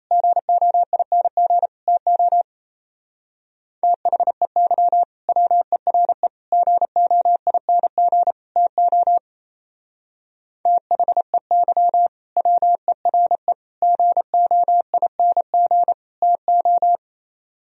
Morse code, heard as a keyed tone, about 34 wpm, 700 Hz. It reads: GOING TO THEY WERE GOING TO THEY WERE GOING TO
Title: Speed-Racing - Sets of 4 Words 28wpm